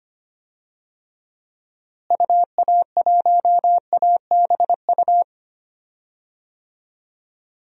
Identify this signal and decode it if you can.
Morse code, heard as a keyed tone, about 25 wpm, 700 Hz. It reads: UA1ABU
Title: DXCC-100 Call Signs - 6 Characters - Repeated 2x - 25wpm